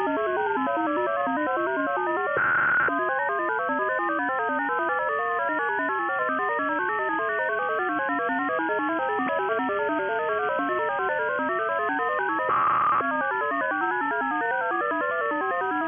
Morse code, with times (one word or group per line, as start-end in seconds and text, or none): none